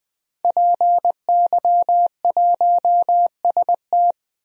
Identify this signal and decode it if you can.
Morse code, heard as a keyed tone, about 20 wpm, 700 Hz. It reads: PY1ST
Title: DXCC-100 Call Signs - 5 Characters - 20wpm